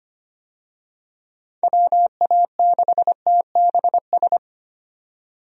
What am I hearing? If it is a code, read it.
Morse code, heard as a keyed tone, about 25 wpm, 700 Hz. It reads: WA6TBS
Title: USA Call Signs - 6 Characters - 25wpm